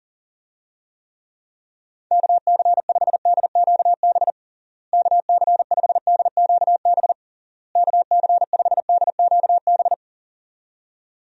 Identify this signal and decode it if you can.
Morse code, heard as a keyed tone, about 40 wpm, 700 Hz. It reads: KC5DQB KC5DQB KC5DQB
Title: USA Call Signs - 6 Characters - Repeated 3x - 40wpm